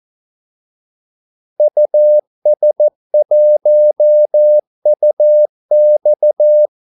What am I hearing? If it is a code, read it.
Morse code, heard as a keyed tone, about 14 wpm, 600 Hz. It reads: US1UX
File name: Callsigns-14wpm